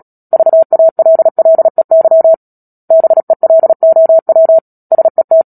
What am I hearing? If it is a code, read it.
Morse code, heard as a keyed tone, about 37 wpm, 650 Hz. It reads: VALLEY BELOW SET
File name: Book Two / 10-Chapter 10 - Epilogue - 36 WPM